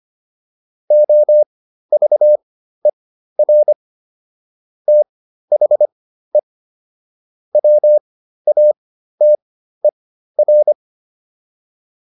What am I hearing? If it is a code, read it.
Morse code, heard as a keyed tone, about 25 wpm, spaced as for 13 wpm, 600 Hz. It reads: OVER THE WATER